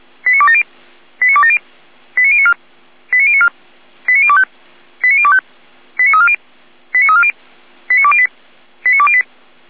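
Five-tone selective call (ZVEI), heard as quick runs of five tones, tones of about 70 ms. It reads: 89270 89270 89094 89094 89025 89025 89340 89340 89208 89208
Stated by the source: ZVEI five-tone selective call received on SDR.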